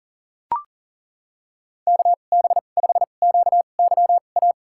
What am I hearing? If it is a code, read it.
Morse code, heard as a keyed tone, about 40 wpm, spaced as for 29 wpm, 700 Hz. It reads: KB5QYA
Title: USA Call Signs - 2x Character Spacing - 40wpm